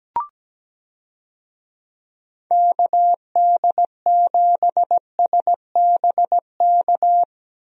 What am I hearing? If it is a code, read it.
Morse code, heard as a keyed tone, about 17 wpm, 700 Hz. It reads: KD7SBK
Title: USA Call Signs - 6 Characters - Repeated 3x - 17wpm